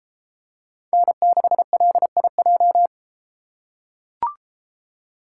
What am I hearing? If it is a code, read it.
Morse code, heard as a keyed tone, about 31 wpm, 700 Hz. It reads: N6LIJ